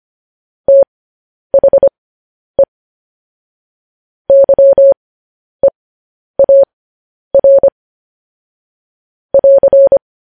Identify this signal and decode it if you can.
Morse code, heard as a keyed tone, about 25 wpm, spaced as for 10 wpm, 550 Hz. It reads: THE YEAR <AR>
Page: news headlines in morse code at 10 wpm